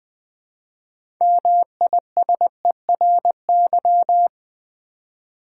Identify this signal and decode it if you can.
Morse code, heard as a keyed tone, about 20 wpm, 700 Hz. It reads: MISERY
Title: Build A Word - 2 Letters At A Time - 6-Letter Words - 20wpm